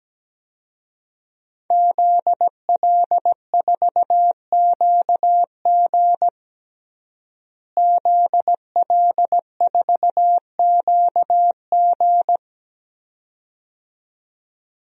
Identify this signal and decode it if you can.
Morse code, heard as a keyed tone, about 17 wpm, 700 Hz. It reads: ZL4QG ZL4QG
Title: DXCC-100 Call Signs - 5 Characters - Repeated 2x - 17wpm